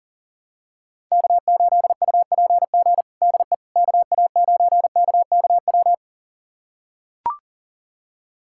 Morse code, heard as a keyed tone, about 37 wpm, 700 Hz. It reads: K8UPG DE KA9KKW
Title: USA Callsign DE Callsign 40wpm